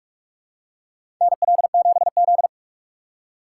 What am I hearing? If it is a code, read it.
Morse code, heard as a keyed tone, about 46 wpm, 700 Hz. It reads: NL7Z
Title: USA Call Signs - 4 Characters - 4x Character Spacing - 45wpm